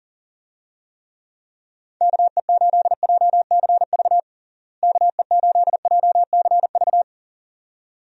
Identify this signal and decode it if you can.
Morse code, heard as a keyed tone, about 40 wpm, 700 Hz. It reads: KE8JCV KE8JCV
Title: USA Call Signs - 6 Characters - Repeated 2x - 40wpm